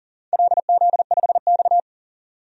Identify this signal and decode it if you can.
Morse code, heard as a keyed tone, about 40 wpm, 700 Hz. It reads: LZ5X
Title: DXCC-100 Call Signs - 4 Characters - 40wpm